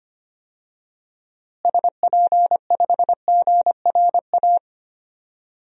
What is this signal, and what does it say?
Morse code, heard as a keyed tone, about 25 wpm, 700 Hz. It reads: SP5GRA
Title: DXCC-100 Call Signs - 6 Characters - Repeated 2x - 25wpm